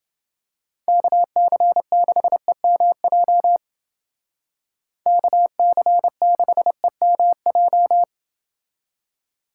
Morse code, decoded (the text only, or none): KC6EMJ KC6EMJ